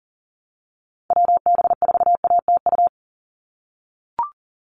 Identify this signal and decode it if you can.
Morse code, heard as a keyed tone, about 40 wpm, 690 Hz. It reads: WB4ATU
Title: USA Call Signs 40wpm